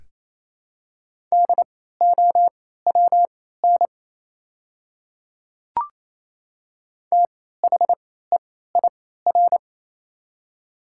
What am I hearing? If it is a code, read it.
Morse code, heard as a keyed tone, about 27 wpm, spaced as for 16 wpm, 700 Hz. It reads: DOWN THEIR
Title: 3x Character Spacing - Top 300 Words - 28wpm